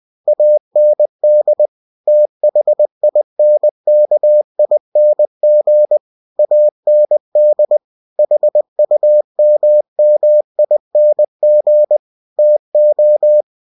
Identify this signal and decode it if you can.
Morse code, heard as a keyed tone, about 20 wpm, 600 Hz. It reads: AND THINKING AND HUMMING TO
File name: cw-books / winnie-the-pooh / winnie-20wpm-page-0017